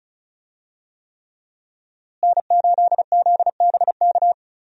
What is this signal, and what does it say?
Morse code, heard as a keyed tone, about 34 wpm, 700 Hz. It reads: N8ZBK
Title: USA Call Signs - 5 Characters - Repeated 3x - 35wpm